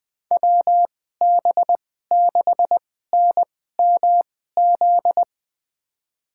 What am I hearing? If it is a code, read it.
Morse code, heard as a keyed tone, about 20 wpm, spaced as for 14 wpm, 700 Hz. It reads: WB6NMZ
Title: USA Call Signs - 6 Characters - 2x Character Spacing - 20wpm